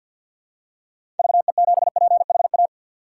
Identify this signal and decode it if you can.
Morse code, heard as a keyed tone, about 46 wpm, 700 Hz. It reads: VE7PHA
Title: K1USN SST QSO Practice - 50wpm